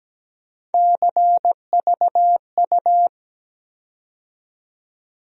Morse code, heard as a keyed tone, about 17 wpm, 700 Hz. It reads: CVU
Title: USA Call Sign Postfixes - 3 Letters 17wpm